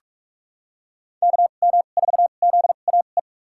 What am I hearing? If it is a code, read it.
Morse code, heard as a keyed tone, about 43 wpm, spaced as for 32 wpm, 700 Hz. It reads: KM4ZAE
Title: USA Call Signs - 2x Character Spacing - 45wpm